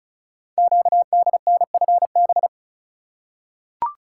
Morse code, heard as a keyed tone, about 36 wpm, 700 Hz. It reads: QDNFB
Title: Code Groups - 5 Letters - 35wpm